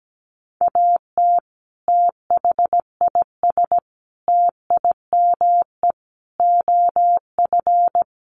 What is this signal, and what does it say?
Morse code, heard as a keyed tone, about 17 wpm, 700 Hz. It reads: AT THIS TIME OF
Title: Sets of 5 Words Encore2 17wpm